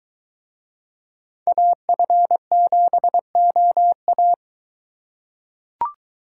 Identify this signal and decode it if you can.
Morse code, heard as a keyed tone, about 23 wpm, 700 Hz. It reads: AF7OA